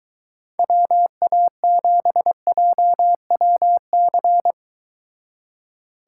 Morse code, heard as a keyed tone, about 22 wpm, 700 Hz. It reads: WA7JWC